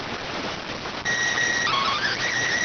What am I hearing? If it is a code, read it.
SSTV (slow-scan television), PD 290.